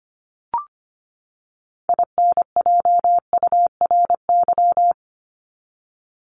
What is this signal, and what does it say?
Morse code, heard as a keyed tone, about 25 wpm, 700 Hz. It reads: INJURY
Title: Top 1751-2000 Words 25wpm